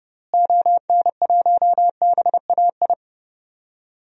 Morse code, heard as a keyed tone, about 31 wpm, 700 Hz. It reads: ON1BAI
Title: DXCC-100 Call Signs - 6 Characters - Repeated 2x - 30wpm